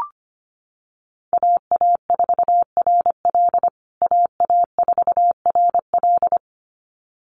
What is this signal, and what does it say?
Morse code, heard as a keyed tone, about 25 wpm, 700 Hz. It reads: AA4RL AA4RL
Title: USA Call Signs Repeated 2x 25wpm